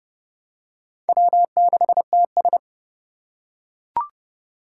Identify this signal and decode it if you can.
Morse code, heard as a keyed tone, about 29 wpm, 700 Hz. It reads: W6TS